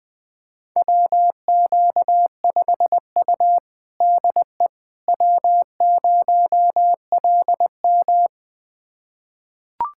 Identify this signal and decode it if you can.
Morse code, heard as a keyed tone, about 20 wpm, 700 Hz. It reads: WQ5U DE W0LM
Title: USA Callsign DE Callsign - 4 Characters - 20wpm